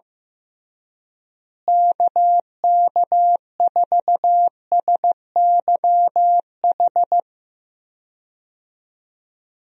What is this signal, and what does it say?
Morse code, heard as a keyed tone, about 15 wpm, 700 Hz. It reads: KK4SYH
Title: USA Call Signs - 6 Characters - Repeated 3x - 15wpm